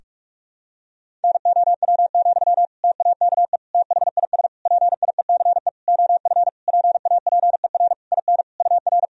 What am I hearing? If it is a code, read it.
Morse code, heard as a keyed tone, about 42 wpm, 700 Hz. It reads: NOW, TAKE THIS PIECE OF PAPER IN UR
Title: Sentences from Top 400 Words and Abbreviations 45wpm